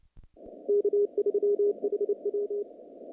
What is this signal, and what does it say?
Morse code, heard as a keyed tone, about 29 wpm, 410 Hz. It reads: K3HW
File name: contests / naqp-cw-k5tr-op-k5pi- -2200